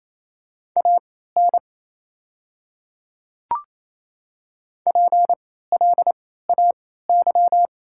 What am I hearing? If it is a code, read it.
Morse code, heard as a keyed tone, about 28 wpm, spaced as for 16 wpm, 700 Hz. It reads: AN PLAY
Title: 3x Character Spacing - Top 500 Words - 28wpm